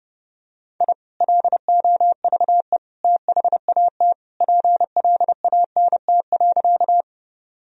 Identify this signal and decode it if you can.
Morse code, heard as a keyed tone, about 30 wpm, 700 Hz. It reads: I LOVE THAT PLANT.
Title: Sentences from Top 1000 Words - Review Part 1 - 30wpm